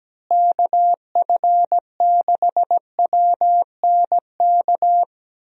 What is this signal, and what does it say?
Morse code, heard as a keyed tone, about 17 wpm, 700 Hz. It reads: KF6WNK
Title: USA Call Signs - 6 Characters - Repeated 3x - 17wpm